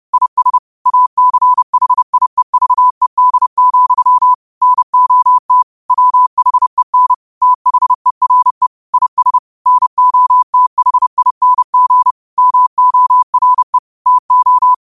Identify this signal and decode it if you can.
Morse code, heard as a keyed tone, about 30 wpm, 1000 Hz. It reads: IS ACHIEVED, NOT WHEN THERE IS NOTHING MORE TO